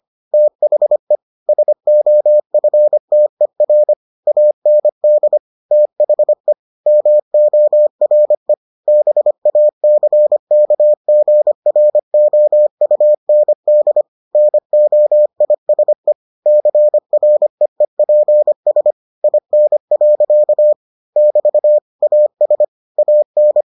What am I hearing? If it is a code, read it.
Morse code, heard as a keyed tone, about 25 wpm, 600 Hz. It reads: THE SOFTER AND THE MORE BACKGROUND NOISE CREEPS IN. <BT> AS AN